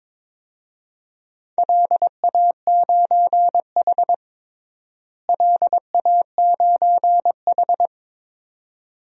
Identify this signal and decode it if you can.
Morse code, heard as a keyed tone, about 21 wpm, 700 Hz. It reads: LA9H LA9H
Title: DXCC-100 Call Signs - 4 Characters - Repeated 2x - 22wpm